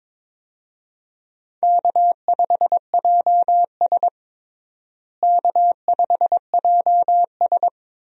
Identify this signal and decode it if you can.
Morse code, heard as a keyed tone, about 22 wpm, 700 Hz. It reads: K5JS K5JS
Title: USA Call Signs - 4 Characters - Repeated 3x - 22wpm